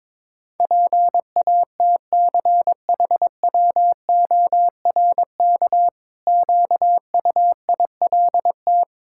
Morse code, heard as a keyed tone, about 22 wpm, 700 Hz. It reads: PATCHWORK QUILT
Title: Pangram Challenge v2 - 22wpm